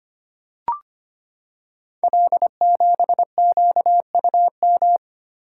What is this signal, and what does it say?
Morse code, heard as a keyed tone, about 25 wpm, 700 Hz. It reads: L7QUM
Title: Code Groups - 5 Letter:Numbers - 25wpm